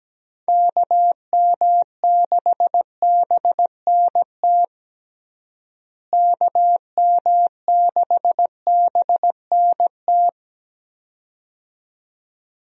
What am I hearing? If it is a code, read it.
Morse code, heard as a keyed tone, about 17 wpm, 700 Hz. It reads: KM6BNT KM6BNT
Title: USA Call Signs - 6 Characters - Repeated 2x - 17wpm